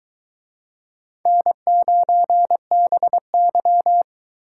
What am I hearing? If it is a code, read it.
Morse code, heard as a keyed tone, about 23 wpm, 700 Hz. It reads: N9BY